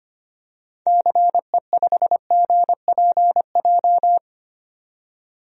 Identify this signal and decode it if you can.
Morse code, heard as a keyed tone, about 25 wpm, 700 Hz. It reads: CE5GPJ
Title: DXCC-100 Call Signs - 6 Characters - 25wpm